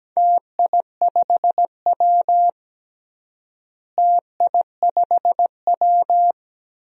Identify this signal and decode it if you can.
Morse code, heard as a keyed tone, about 17 wpm, 700 Hz. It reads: TI5W TI5W
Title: DXCC-100 Call Signs - 4 Characters - Repeated 2x - 17wpm